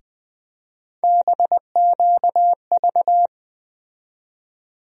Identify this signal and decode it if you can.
Morse code, heard as a keyed tone, about 20 wpm, 700 Hz. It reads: BQV
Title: USA Call Sign Postfixes - 3 Letters 20wpm